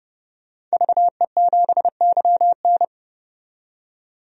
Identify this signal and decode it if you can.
Morse code, heard as a keyed tone, about 30 wpm, 700 Hz. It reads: VE7YN
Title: DXCC-100 Call Signs - 5 Characters - 30wpm